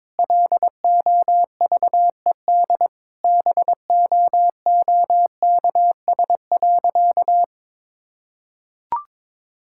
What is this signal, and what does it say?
Morse code, heard as a keyed tone, about 22 wpm, 700 Hz. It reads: LOVED BOOKS.
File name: practice / 22wpm / Sentences with 3-Word Phrases - 22wpm